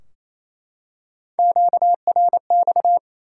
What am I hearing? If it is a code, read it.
Morse code, heard as a keyed tone, about 28 wpm, 700 Hz. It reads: QRX
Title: Top 180 QSO Elements 28wpm